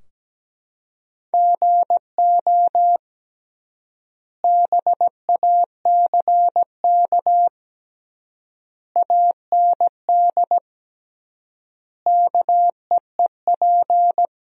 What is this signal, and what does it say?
Morse code, heard as a keyed tone, about 17 wpm, 700 Hz. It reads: GO BACK AND KEEP
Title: 3x Word-Spacing - Sentences from Top 300 Words - 17wpm